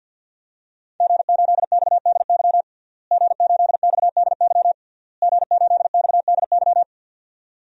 Morse code, heard as a keyed tone, about 49 wpm, 700 Hz. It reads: G8XDY G8XDY G8XDY
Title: DXCC-100 Call Signs - 5 Characters - Repeated 3x - 50wpm